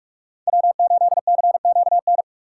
Morse code, heard as a keyed tone, about 48 wpm, 700 Hz. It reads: W8CQN
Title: USA Call Signs - 5 Characters - 2x Character Spacing - 45wpm